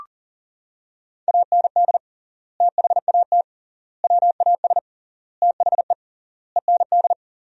Morse code, heard as a keyed tone, about 40 wpm, 700 Hz. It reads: AND THAT WAS THE END
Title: 3x Word-Spacing - Sentences from Top 200 Words - 40wpm